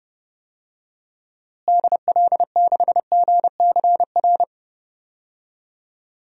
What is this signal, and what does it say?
Morse code, heard as a keyed tone, about 32 wpm, 700 Hz. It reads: DL6GCR
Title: DXCC-100 Call Signs - 6 Characters - Repeated 3x - 30wpm